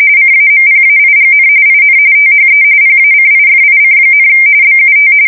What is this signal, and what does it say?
RTTY, about 45.45 baud. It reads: CAMPAIGN IN THE PACIFIC, BUT
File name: fsk170 / usnavy1